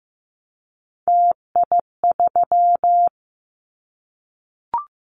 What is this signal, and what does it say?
Morse code, heard as a keyed tone, about 15 wpm, 700 Hz. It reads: TI3
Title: Callsign Prefixes for Top 100 DXCC - 15wpm